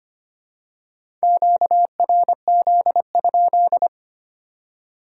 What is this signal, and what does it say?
Morse code, heard as a keyed tone, about 25 wpm, 700 Hz. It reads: QRZ?
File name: Top 90 QSO Elements - 4x Character Spacing - 25wpm